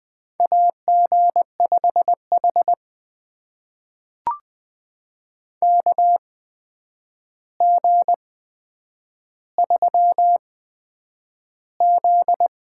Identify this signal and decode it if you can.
Morse code, heard as a keyed tone, about 20 wpm, spaced as for 9 wpm, 700 Hz. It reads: AG5H KG3Z
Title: USA Call Signs - 4 Characters - 8x Character Spacing - 20wpm